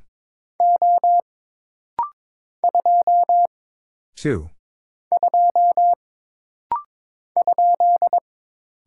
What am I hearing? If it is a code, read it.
Morse code, heard as a keyed tone, about 22 wpm, 700 Hz. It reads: O 2 2 ?
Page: This rapid-fire timing is geared to push your ICR (Instant Character Recognition) skills.